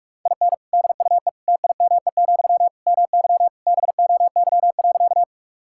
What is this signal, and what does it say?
Morse code, heard as a keyed tone, about 45 wpm, 700 Hz. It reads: IN DUE TIME, MY BOY.
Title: Sentences from Top 1000 Words - Review Part 6 - 45wpm